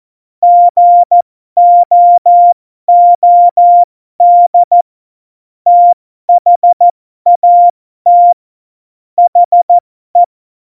Morse code, heard as a keyed tone, about 14 wpm, spaced as for 12 wpm, 700 Hz. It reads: GOOD THAT HE
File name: Morsekurs / Witze